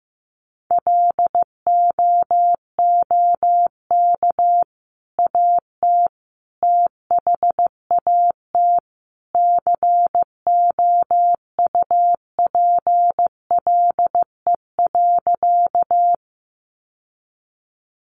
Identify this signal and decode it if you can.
Morse code, heard as a keyed tone, about 15 wpm, 700 Hz. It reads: LOOK AT THAT COUPLE.
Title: Sentences from Top 1100 Words 15wpm